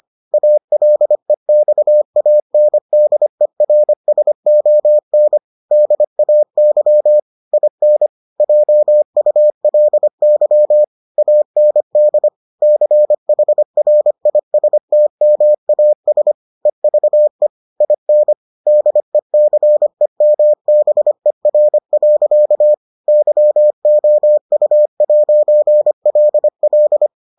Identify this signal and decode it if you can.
Morse code, heard as a keyed tone, about 25 wpm, 600 Hz. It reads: ALEXANDERSON DAY IN JULY AND CHRISTMAS EVE IN DECEMBER. YOU'LL